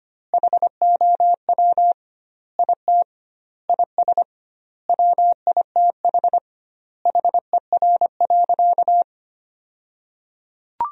Morse code, heard as a keyed tone, about 25 wpm, 700 Hz. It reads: HOW IT IS WITH HER.
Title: 2x Word-Spacing - Sentences from Top 100 Words - 25wpm